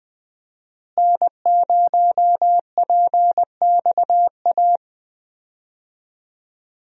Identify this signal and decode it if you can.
Morse code, heard as a keyed tone, about 20 wpm, 700 Hz. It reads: N0PXA
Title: USA Call Signs - 5 Characters - Repeated 3x - 20wpm